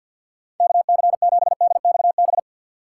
Morse code, heard as a keyed tone, about 47 wpm, 700 Hz. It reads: KC7DXB
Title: USA Call Signs - 6 Characters - 6x Character Spacing - 50wpm